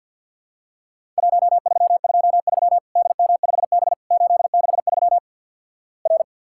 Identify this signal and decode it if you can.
Morse code, heard as a keyed tone, about 50 wpm, 700 Hz. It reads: 1323 DM5B 863 R
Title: Worked All Europe DX Contest - QTC Exchange - 50wpm